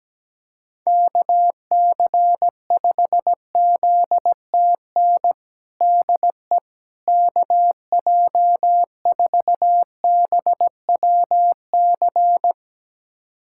Morse code, heard as a keyed tone, about 17 wpm, 700 Hz. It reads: KC5ZTN DE KJ4BWC
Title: USA Callsign DE Callsign 17wpm